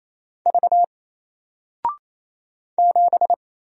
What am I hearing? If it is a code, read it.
Morse code, heard as a keyed tone, about 28 wpm, 700 Hz. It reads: V 7
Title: Single Letter-Number Alt 28wpm